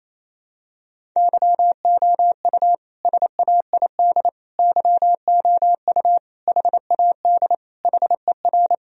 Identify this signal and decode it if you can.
Morse code, heard as a keyed tone, about 28 wpm, 700 Hz. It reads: YOU SAID YOU HAD HER
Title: Sentences from Top 1000 Words - Review Part 1 - 28wpm